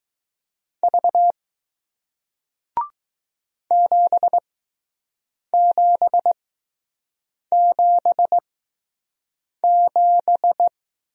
Morse code, heard as a keyed tone, about 20 wpm, spaced as for 6 wpm, 700 Hz. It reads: V 7777